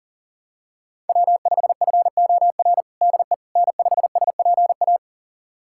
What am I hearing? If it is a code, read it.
Morse code, heard as a keyed tone, about 40 wpm, 700 Hz. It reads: W5FOR DE N5SPA
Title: USA Callsign DE Callsign - 5 Characters - 40wpm